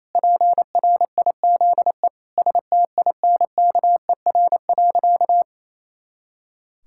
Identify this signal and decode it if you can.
Morse code, heard as a keyed tone, about 28 wpm, 700 Hz. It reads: PRIZE STINKER.
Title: Pangram Challenge v2 - 28wpm